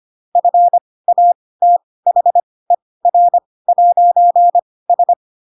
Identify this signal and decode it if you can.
Morse code, heard as a keyed tone, about 25 wpm, spaced as for 18 wpm, 700 Hz. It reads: FATHER'S